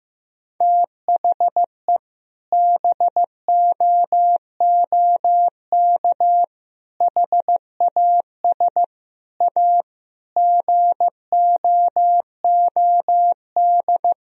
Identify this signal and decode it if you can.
Morse code, heard as a keyed tone, about 15 wpm, 700 Hz. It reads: THE BOOK HAS A GOOD
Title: Sentences from Top 1000 Words - Review Part 3 - 15wpm